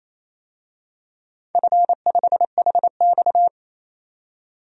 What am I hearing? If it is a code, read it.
Morse code, heard as a keyed tone, about 29 wpm, 690 Hz. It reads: F5HX
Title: DXCC-100 Call Signs - 4 Characters - 28wpm